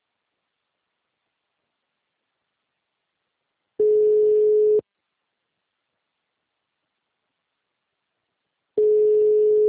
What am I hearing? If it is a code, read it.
Ringback tone (Europe, 425 Hz).